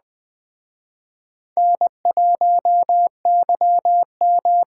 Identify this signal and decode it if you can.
Morse code, heard as a keyed tone, about 21 wpm, 700 Hz. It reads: N1YM